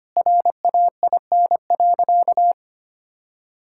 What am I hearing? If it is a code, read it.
Morse code, heard as a keyed tone, about 25 wpm, 700 Hz. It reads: RAIN.
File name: practice / 25wpm / Sentences with 2-Word Phrases - 25wpm